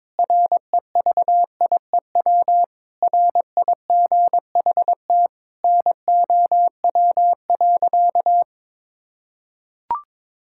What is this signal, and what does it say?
Morse code, heard as a keyed tone, about 21 wpm, 700 Hz. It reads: REVIEW RIGHT NOW.
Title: Sentences from Top 1000 Words - Review Part 6 - 22wpm